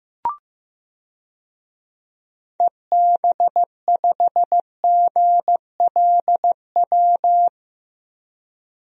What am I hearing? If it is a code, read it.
Morse code, heard as a keyed tone, about 15 wpm, 700 Hz. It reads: EB5GLW
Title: DXCC-100 Call Signs - 6 Characters - Repeated 3x - 15wpm